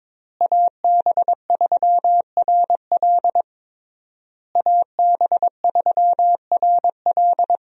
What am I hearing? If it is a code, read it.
Morse code, heard as a keyed tone, about 22 wpm, 700 Hz. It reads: AB3RL AB3RL
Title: USA Call Signs - 5 Characters - Repeated 3x - 22wpm